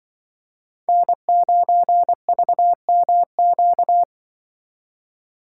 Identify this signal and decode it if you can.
Morse code, heard as a keyed tone, about 24 wpm, 700 Hz. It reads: N9VMQ
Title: Speed Racing - USA 1x3 Call Signs 20wpm